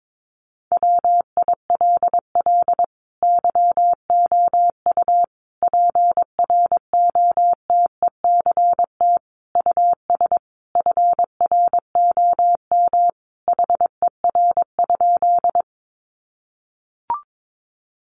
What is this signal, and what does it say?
Morse code, heard as a keyed tone, about 22 wpm, 700 Hz. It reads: WILL YOU PROTECT US FROM HER?
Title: Sentences from Top 900 Words 22wpm